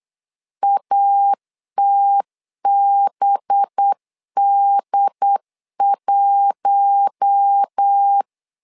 Morse code, heard as a keyed tone, about 8 wpm, 800 Hz. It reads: ATBD1
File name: LEZIONI / 5 LBVC / 5 LB 8wpm 2816